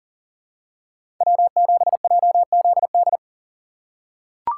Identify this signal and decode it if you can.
Morse code, heard as a keyed tone, about 40 wpm, 700 Hz. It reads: W7JZD
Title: USA Call Signs - 5 Characters - 2x Character Spacing - 40wpm